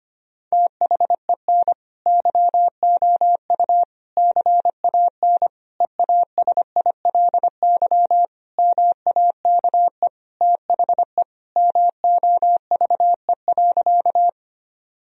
Morse code, heard as a keyed tone, about 25 wpm, 700 Hz. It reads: THEN YOU CAN EASILY MAKE THE MOVE.
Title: Sentences from Top 1000 Words - Review Part 1 - 25wpm